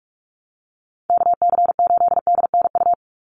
Morse code, heard as a keyed tone, about 44 wpm, 700 Hz. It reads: KC8DNU
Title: USA Call Signs 45wpm